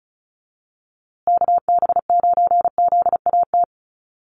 Morse code, heard as a keyed tone, about 34 wpm, 700 Hz. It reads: KB9ZAT
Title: USA Call Signs Repeated 3x 35wpm